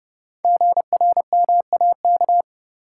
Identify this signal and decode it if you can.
Morse code, heard as a keyed tone, about 29 wpm, 700 Hz. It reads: GRMAK
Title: Code Groups - 5 Letters - 30wpm